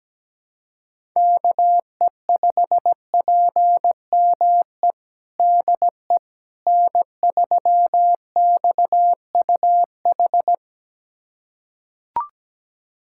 Morse code, heard as a keyed tone, about 17 wpm, 700 Hz. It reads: KE5PME DE N3XUH
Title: USA Callsign DE Callsign 17wpm